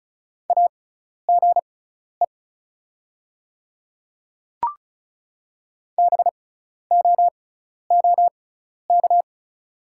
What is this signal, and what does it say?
Morse code, heard as a keyed tone, about 34 wpm, spaced as for 12 wpm, 690 Hz. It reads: AGE BOOK
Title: 6x Character Spacing - Top 500 Words - 35wpm